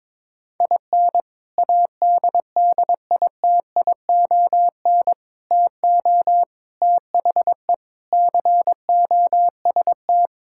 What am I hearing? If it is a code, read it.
Morse code, heard as a keyed tone, about 22 wpm, 700 Hz. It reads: IN ADDITION TO THE COST
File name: practice / 22wpm / Sentences with 4-Word Phrases - 22wpm